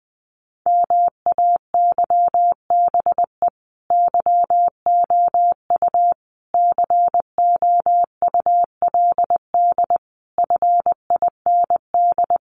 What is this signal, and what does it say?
Morse code, heard as a keyed tone, about 20 wpm, 700 Hz. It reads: MAYBE YOU COULD FIND
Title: Sentences from Top 1400 Words 20wpm